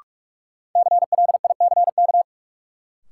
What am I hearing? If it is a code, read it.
Morse code, heard as a keyed tone, about 44 wpm, 700 Hz. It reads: CLICK
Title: Sentences from Top 1100 Words - New Word Drill - 45wpm